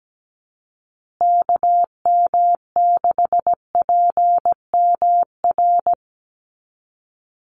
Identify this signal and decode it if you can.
Morse code, heard as a keyed tone, about 17 wpm, 700 Hz. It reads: KM6PMR
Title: USA Call Signs Repeated 2x 17wpm